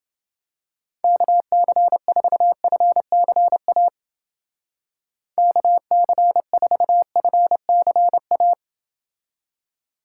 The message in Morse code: KC4FCA KC4FCA

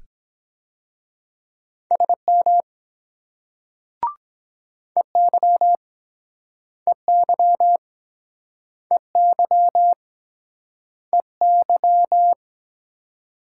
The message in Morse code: SM EY EY EY EY

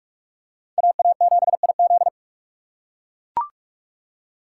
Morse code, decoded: AA7IZ

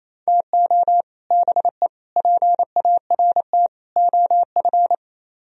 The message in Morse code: TO BE PART OF

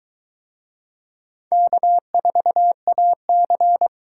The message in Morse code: K4AC

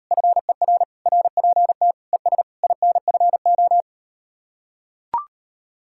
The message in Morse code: FER RPT ES INFO

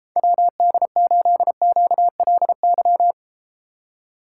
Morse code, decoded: WD8QLY

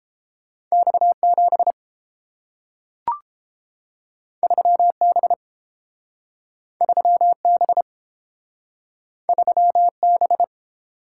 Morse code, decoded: X7 3B 3B 3B

